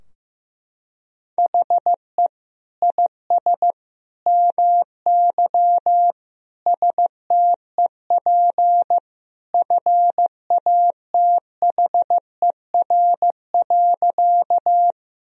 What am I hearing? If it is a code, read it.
Morse code, heard as a keyed tone, about 15 wpm, 700 Hz. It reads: HE IS MY STEP FATHER.
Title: Sentences from Top 1000 Words - Review Part 2 - 15wpm